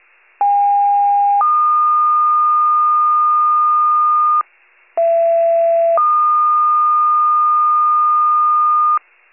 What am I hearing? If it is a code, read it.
Two-tone page, heard as a one-second tone, then a three-second tone, about 810 then 1220 Hz, about 670 then 1220 Hz. It is heard from 0.4 to 9.0 s.